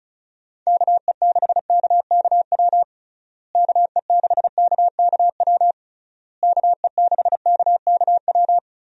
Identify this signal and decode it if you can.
Morse code, heard as a keyed tone, about 36 wpm, 700 Hz. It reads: KE6KKW KE6KKW KE6KKW
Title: USA Call Signs - 6 Characters - Repeated 3x - 35wpm